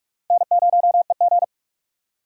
Morse code, heard as a keyed tone, about 44 wpm, 700 Hz. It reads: N0EG